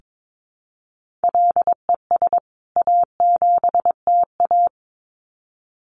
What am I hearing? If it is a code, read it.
Morse code, heard as a keyed tone, about 22 wpm, 700 Hz. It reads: LES A7TA
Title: CWT Exchange 22wpm